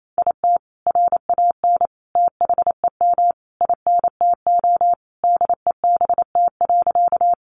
Morse code, heard as a keyed tone, about 28 wpm, 700 Hz. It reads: IT RAN THEM INTO DEBT.
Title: Sentences from Top 1300 Words 28wpm